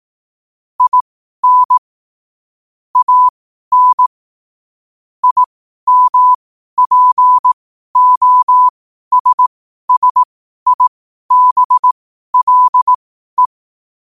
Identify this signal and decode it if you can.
Morse code, heard as a keyed tone, about 18 wpm, spaced as for 12 wpm, 1000 Hz. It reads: IN AN IMPOSSIBLE
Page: New quotes every day in morse code at 12 Words per minute.